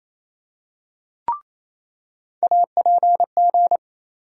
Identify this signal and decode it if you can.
Morse code, heard as a keyed tone, about 27 wpm, 700 Hz. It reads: APG
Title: USA Call Sign Postfixes - 3 Letters 28wpm